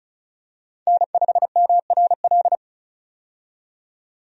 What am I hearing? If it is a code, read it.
Morse code, heard as a keyed tone, about 34 wpm, 700 Hz. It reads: N5MRL
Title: USA Call Signs - 5 Characters - 35wpm